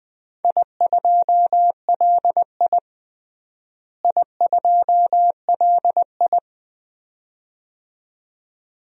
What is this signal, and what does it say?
Morse code, heard as a keyed tone, about 20 wpm, 700 Hz. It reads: I2LI I2LI
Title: DXCC-100 Call Signs - 4 Characters - Repeated 2x - 20wpm